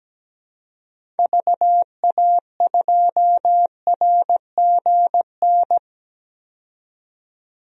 Morse code, heard as a keyed tone, about 17 wpm, 700 Hz. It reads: VA2RGN
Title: DXCC-100 Call Signs - 6 Characters - 17wpm